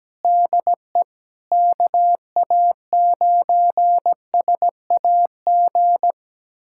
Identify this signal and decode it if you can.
Morse code, heard as a keyed tone, about 17 wpm, 700 Hz. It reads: DE KA9SAG
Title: USA Callsign DE Callsign - 6 Characters - 17wpm